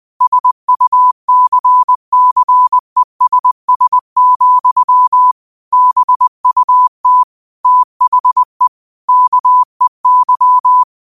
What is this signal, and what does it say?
Morse code, heard as a keyed tone, about 20 wpm, 1000 Hz. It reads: SUCCESS, BUT THE KEY
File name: qotdfiles / 20wpm